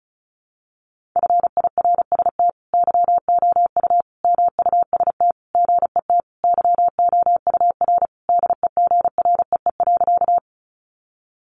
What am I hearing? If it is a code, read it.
Morse code, heard as a keyed tone, about 34 wpm, 700 Hz. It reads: FIRST YOU MUST GET YOUR DEGREE.
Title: Sentences from Top 1500 Words 35wpm